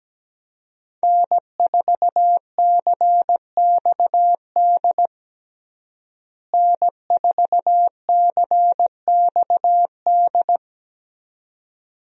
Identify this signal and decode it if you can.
Morse code, heard as a keyed tone, about 17 wpm, 700 Hz. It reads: N4CXD N4CXD
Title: USA Call Signs - 5 Characters - Repeated 2x - 17wpm